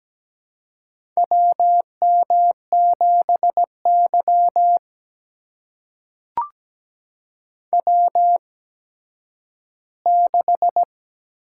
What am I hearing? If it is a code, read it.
Morse code, heard as a keyed tone, about 17 wpm, spaced as for 10 wpm, 700 Hz. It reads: WM7Y W6
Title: USA Call Signs - 4 Characters - 8x Character Spacing - 17wpm